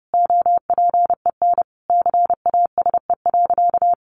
Morse code, heard as a keyed tone, about 30 wpm, 700 Hz. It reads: OPEN CASE.
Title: Sentences from Top 1000 Words 30wpm